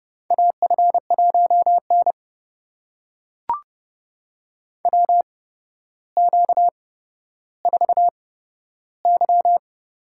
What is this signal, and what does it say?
Morse code, heard as a keyed tone, about 30 wpm, spaced as for 13 wpm, 700 Hz. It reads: AF1N WQ4Y
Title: USA Call Signs - 4 Characters - 8x Character Spacing - 30wpm